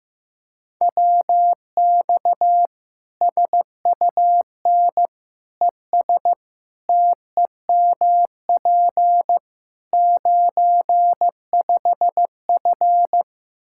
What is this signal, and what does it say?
Morse code, heard as a keyed tone, about 15 wpm, 700 Hz. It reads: WX SUN ES TEMP 95F
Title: K1USN SST QSO Practice 15wpm